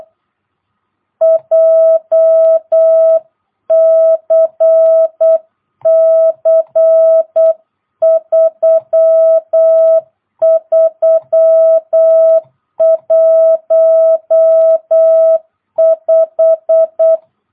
Morse code, heard as a keyed tone, about 8 wpm, 660 Hz. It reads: JCC 3315
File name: jh8cbh / 2022cw